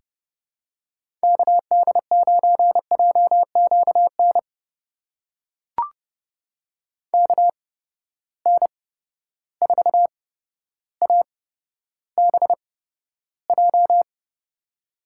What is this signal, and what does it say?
Morse code, heard as a keyed tone, about 30 wpm, spaced as for 13 wpm, 700 Hz. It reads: KD9JQN KN4ABJ
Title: USA Call Signs - 6 Characters - 8x Character Spacing - 30wpm